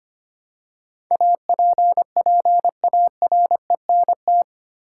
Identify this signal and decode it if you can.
Morse code, heard as a keyed tone, about 25 wpm, 700 Hz. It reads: APPARENT
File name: 8 Letter Words - First 3 Letters Given - 25wpm